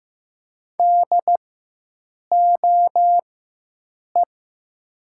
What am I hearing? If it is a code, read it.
Morse code, heard as a keyed tone, about 15 wpm, spaced as for 7 wpm, 700 Hz. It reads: DOE
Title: 4x Character Spacing - Top 300 Words - 15wpm